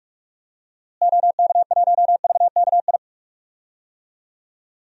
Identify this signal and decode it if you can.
Morse code, heard as a keyed tone, about 46 wpm, 700 Hz. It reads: OK1VKI
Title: DXCC-100 Call Signs - 6 Characters - Repeated 2x - 45wpm